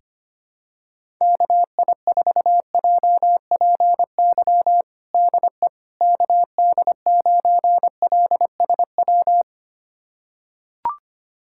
Morse code, heard as a keyed tone, about 25 wpm, 700 Hz. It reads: KI4JPY DE KD9LSW
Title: USA Callsign DE Callsign 25wpm